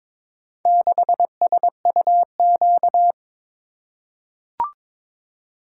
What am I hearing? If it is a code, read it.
Morse code, heard as a keyed tone, about 22 wpm, 700 Hz. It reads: 6SUQ
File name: practice / 22wpm / Four Letter-Number Combinations - 22wpm